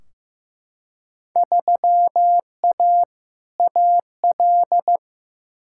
Morse code, heard as a keyed tone, about 15 wpm, 700 Hz. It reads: 3A AL
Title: Field Day Exchanges - 15wpm